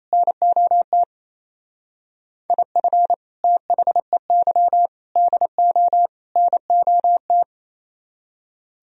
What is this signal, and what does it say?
Morse code, heard as a keyed tone, about 37 wpm, 700 Hz. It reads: NOT IF THEY DO NOT